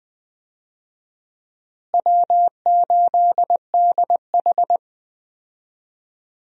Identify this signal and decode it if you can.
Morse code, heard as a keyed tone, about 21 wpm, 700 Hz. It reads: W8DH
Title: USA Call Signs - 4 Characters - Repeated 2x - 20wpm